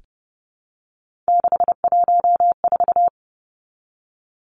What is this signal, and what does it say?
Morse code, heard as a keyed tone, about 30 wpm, 700 Hz. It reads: 614
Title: Numbers - 3-Digit 30wpm